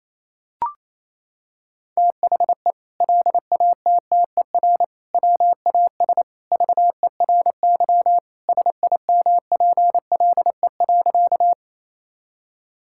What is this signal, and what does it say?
Morse code, heard as a keyed tone, about 28 wpm, 700 Hz. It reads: THE LATTER WAS VERY SIMPLE.
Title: Sentences from Top 1600 Words 28wpm